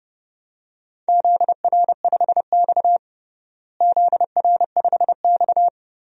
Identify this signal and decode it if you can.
Morse code, heard as a keyed tone, about 30 wpm, 700 Hz. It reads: ZR5X ZR5X
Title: DXCC-100 Call Signs - 4 Characters - Repeated 3x - 30wpm